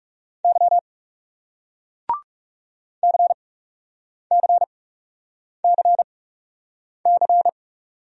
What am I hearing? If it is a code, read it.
Morse code, heard as a keyed tone, about 40 wpm, spaced as for 8 wpm, 700 Hz. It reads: Y CCCC